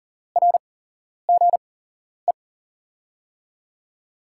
Morse code, heard as a keyed tone, about 38 wpm, spaced as for 11 wpm, 700 Hz. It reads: RGE
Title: USA Call Signs - 5 Characters - 8x Character Spacing - 40wpm